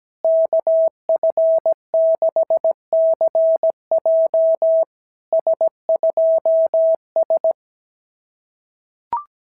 Morse code, heard as a keyed tone, about 17 wpm, 690 Hz. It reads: KF6CJ S2S
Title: SOTA QSO Practice 17wpm